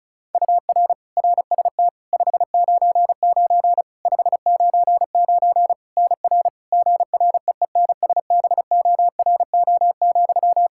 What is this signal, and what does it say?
Morse code, heard as a keyed tone, about 36 wpm, 700 Hz. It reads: UR RST 599 599 NR GREENSBORO,
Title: Practice QSO - Exchange 1 - Beginning - 35wpm